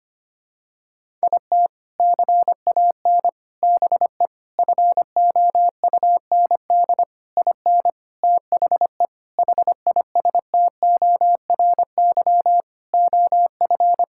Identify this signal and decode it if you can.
Morse code, heard as a keyed tone, about 25 wpm, 700 Hz. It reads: IT CAN BE FOUND IN THE HISTORY OF
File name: practice / 25wpm / Sentences with 4-Word Phrases - 25wpm